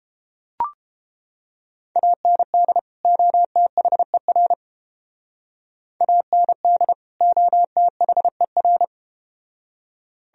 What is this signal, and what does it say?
Morse code, heard as a keyed tone, about 31 wpm, 700 Hz. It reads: AND OTHER AND OTHER